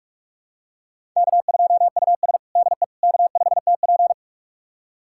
Morse code, heard as a keyed tone, about 44 wpm, 690 Hz. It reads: K2US DE K5TP
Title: USA Callsign DE Callsign - 4 Characters - 45wpm